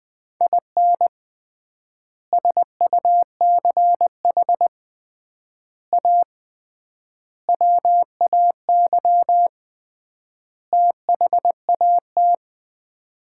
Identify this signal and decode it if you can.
Morse code, heard as a keyed tone, about 20 wpm, 700 Hz. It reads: IN SUCH A WAY THAT
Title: 3x Word-Spacing - Sets of 5 Words - 20wpm